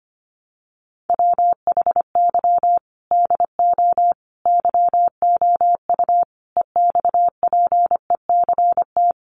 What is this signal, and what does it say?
Morse code, heard as a keyed tone, about 25 wpm, 690 Hz. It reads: WHY DO YOU EXPECT
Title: Sentences from Top 1100 Words 25wpm